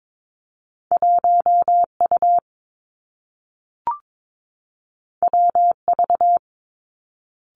Morse code, heard as a keyed tone, about 22 wpm, 700 Hz. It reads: 1 U W V
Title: Two Challenging Letter-Number Combinations 22wpm